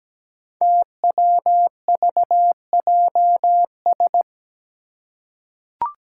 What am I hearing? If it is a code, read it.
Morse code, heard as a keyed tone, about 17 wpm, 700 Hz. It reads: TWVJS
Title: Five Challenging Letter-Number Combinations 17wpm